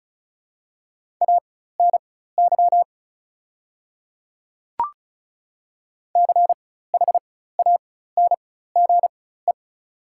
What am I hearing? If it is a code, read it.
Morse code, heard as a keyed tone, about 36 wpm, spaced as for 16 wpm, 700 Hz. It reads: ANY CHANGE
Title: 4x Character Spacing - Top 200 Words - 35wpm